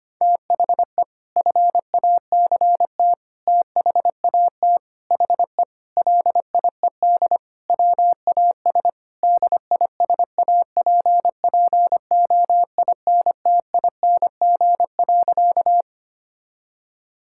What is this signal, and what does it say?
Morse code, heard as a keyed tone, about 25 wpm, 700 Hz. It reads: THE FACT THAT HE LIED WAS DISAPPOINTING.
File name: practice / 25wpm / Sentences with 4-Word Phrases - 25wpm